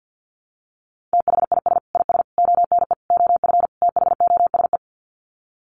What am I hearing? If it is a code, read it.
Morse code, heard as a keyed tone, about 50 wpm, 700 Hz. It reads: THIS IS ONE OF THOSE
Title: Sets of 5 Words Encore2 50wpm